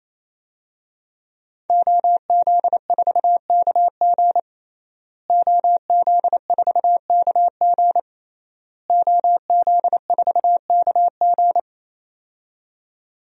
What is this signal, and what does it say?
Morse code, heard as a keyed tone, about 28 wpm, 700 Hz. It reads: OZ4KG OZ4KG OZ4KG
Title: DXCC-100 Call Signs - 5 Characters - Repeated 3x - 28wpm